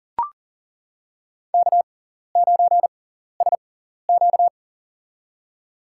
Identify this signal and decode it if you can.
Morse code, heard as a keyed tone, about 40 wpm, spaced as for 14 wpm, 700 Hz. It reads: K9SQ
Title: USA Call Signs - 4 Characters - 6x Character Spacing - 40wpm